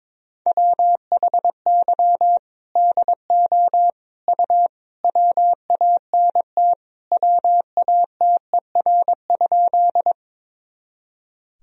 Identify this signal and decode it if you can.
Morse code, heard as a keyed tone, about 21 wpm, 700 Hz. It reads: WHY DO U WANT WATER?
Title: Sentences from Top 200 Words and Abbreviations 22wpm